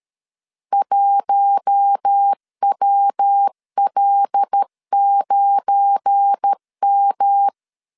Morse code, heard as a keyed tone, about 13 wpm, 800 Hz. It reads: 1WL9M
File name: CW / DET / 1 9 1500C S A 12wpm 3800